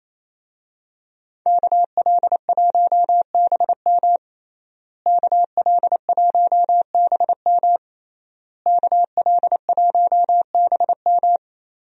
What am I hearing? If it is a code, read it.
Morse code, heard as a keyed tone, about 27 wpm, 700 Hz. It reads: KL1BM KL1BM KL1BM
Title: USA Call Signs - 5 Characters - Repeated 3x - 28wpm